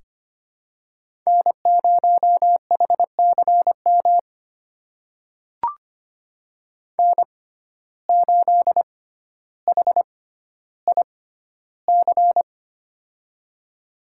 Morse code, heard as a keyed tone, about 25 wpm, spaced as for 13 wpm, 700 Hz. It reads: N0HCM N8HIC
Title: USA Call Signs - 5 Characters - 6x Character Spacing - 25wpm